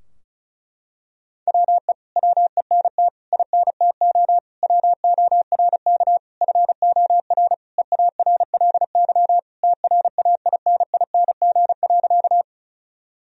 Morse code, heard as a keyed tone, about 36 wpm, 700 Hz. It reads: WE WENT INTO WORK FOR EARLY TRAINING.
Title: Sentences from Top 1000 Words - Review Part 4 - 35wpm